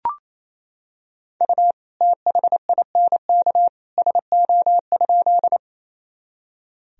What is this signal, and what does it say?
Morse code, heard as a keyed tone, about 27 wpm, 700 Hz. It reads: U THINK SO?
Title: Sentences from Top 200 Words and Abbreviations 28wpm